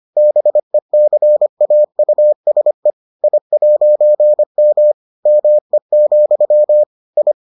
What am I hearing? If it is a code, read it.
Morse code, heard as a keyed tone, about 25 wpm, 600 Hz. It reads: BECAUSE I'M ME, I